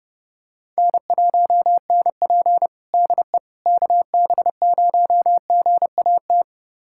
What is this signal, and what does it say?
Morse code, heard as a keyed tone, about 30 wpm, 700 Hz. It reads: N1NP DE KB0GAT
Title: USA Callsign DE Callsign 30wpm